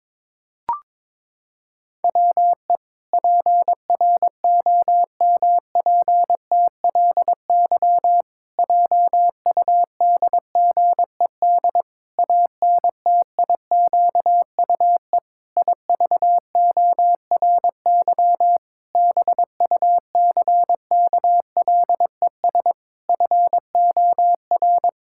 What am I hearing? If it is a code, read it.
Morse code, heard as a keyed tone, about 22 wpm, 700 Hz. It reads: WE PROMPTLY JUDGED ANTIQUE IVORY BUCKLES FOR
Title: Pangram Challenge v2 - 22wpm